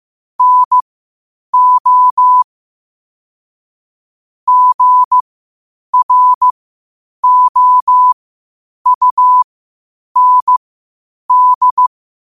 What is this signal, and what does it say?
Morse code, heard as a keyed tone, about 15 wpm, spaced as for 8 wpm, 1000 Hz. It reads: NO GROUND
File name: qotdfiles / 8wpm